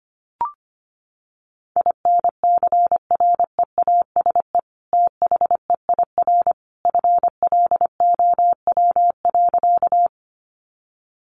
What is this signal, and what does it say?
Morse code, heard as a keyed tone, about 25 wpm, 700 Hz. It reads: INCREASE THEIR FLOW.
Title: Sentences from Top 1200 Words 25wpm